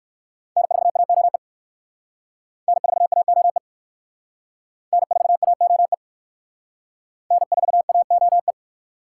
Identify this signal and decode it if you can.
Morse code, heard as a keyed tone, about 56 wpm, 690 Hz. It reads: N4AOE N4AOE N4AOE N4AOE
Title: Speed Racing - USA 1x3 Call Signs 45wpm